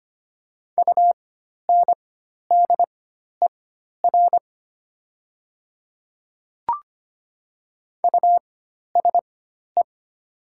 Morse code, heard as a keyed tone, about 25 wpm, spaced as for 12 wpm, 690 Hz. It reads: UNDER USE
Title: 4x Character Spacing - Top 300 Words - 25wpm